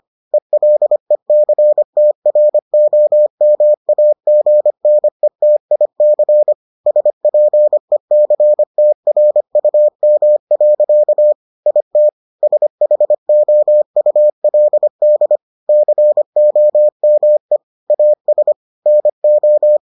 Morse code, heard as a keyed tone, about 25 wpm, 600 Hz. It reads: ELECTROMAGNETIC SPECTRUM. IT SHOULD COME AS NO